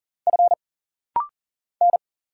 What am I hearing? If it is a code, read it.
Morse code, heard as a keyed tone, about 40 wpm, 700 Hz. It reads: F N
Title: Single Letter-Number - Rapid-Fire 40wpm